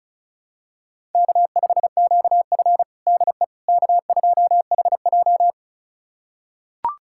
Morse code, heard as a keyed tone, about 36 wpm, 700 Hz. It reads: K5QF DE K2HJ
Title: USA Callsign DE Callsign - 4 Characters - 35wpm